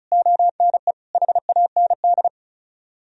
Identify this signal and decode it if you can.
Morse code, heard as a keyed tone, about 47 wpm, spaced as for 41 wpm, 700 Hz. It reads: ONE HAND
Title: Speed-Racing - Sets of 4 Words 35wpm